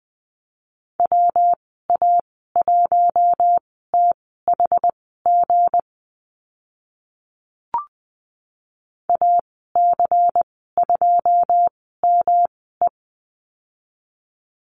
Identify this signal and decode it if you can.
Morse code, heard as a keyed tone, about 20 wpm, spaced as for 14 wpm, 700 Hz. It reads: WA1THG AC2ME
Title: 173 Lesson 28g - Number 9 - Call Signs 20wpm